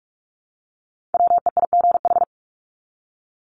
Morse code, heard as a keyed tone, about 44 wpm, 700 Hz. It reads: WEIGH
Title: Top 1751-2000 Words 45wpm